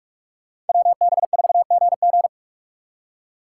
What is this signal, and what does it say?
Morse code, heard as a keyed tone, about 58 wpm, 700 Hz. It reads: WB4GG